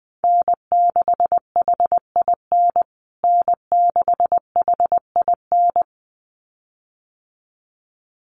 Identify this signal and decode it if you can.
Morse code, heard as a keyed tone, about 20 wpm, 700 Hz. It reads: N6HIN N6HIN
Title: USA Call Signs Repeated 2x 20wpm